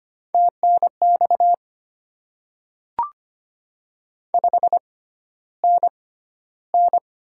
Morse code, heard as a keyed tone, about 25 wpm, spaced as for 13 wpm, 700 Hz. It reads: TNX 5NN
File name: Top 20 QSO Elements - 6x Character Spacing - 25wpm